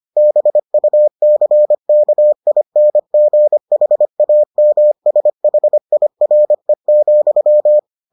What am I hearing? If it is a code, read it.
Morse code, heard as a keyed tone, about 25 wpm, 600 Hz. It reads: BUCKINGHAMSHIRE,